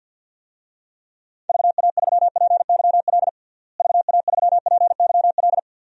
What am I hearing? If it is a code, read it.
Morse code, heard as a keyed tone, about 46 wpm, 700 Hz. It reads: VA3PYL VA3PYL
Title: DXCC-100 Call Signs - 6 Characters - Repeated 3x - 50wpm